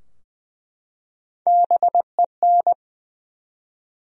Morse code, heard as a keyed tone, about 20 wpm, 700 Hz. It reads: BEN
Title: USA Common Names - Max 3 Letters - 20wpm